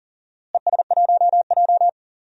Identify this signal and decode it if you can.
Morse code, heard as a keyed tone, about 40 wpm, 700 Hz. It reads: ES1J
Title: DXCC-100 Call Signs - 4 Characters - Repeated 2x - 40wpm